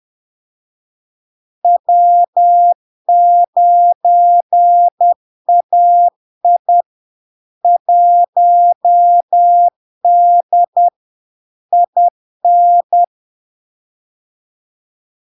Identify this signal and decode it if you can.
Morse code, heard as a keyed tone, about 10 wpm, 700 Hz. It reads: W9AI 1D IN
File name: LARK-FD-2022-Log-in-CW-10wpm-700-Hz